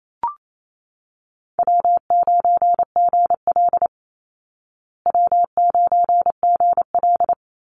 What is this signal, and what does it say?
Morse code, heard as a keyed tone, about 28 wpm, 700 Hz. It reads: W9GL W9GL
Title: Call Signs Repeated 2x - 28wpm